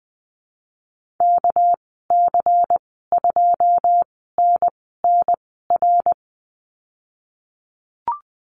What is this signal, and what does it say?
Morse code, heard as a keyed tone, about 19 wpm, 700 Hz. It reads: K C 2 N N R
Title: 238 Lesson 38g - Letter q - Call Signs 20wpm